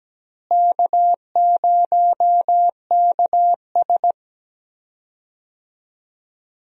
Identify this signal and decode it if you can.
Morse code, heard as a keyed tone, about 17 wpm, 700 Hz. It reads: K0KS
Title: USA Call Signs - 4 Characters - Repeated 2x - 17wpm